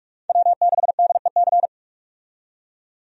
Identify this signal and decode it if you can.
Morse code, heard as a keyed tone, about 47 wpm, 700 Hz. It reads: W6DEC
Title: USA Call Signs - 5 Characters - 45wpm